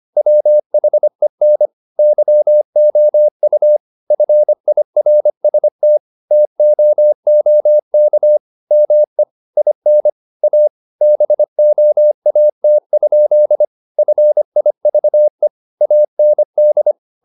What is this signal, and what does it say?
Morse code, heard as a keyed tone, about 25 wpm, 600 Hz. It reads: WHEN YOU FIRST TOOK ME IN A BOAT? FIVE AND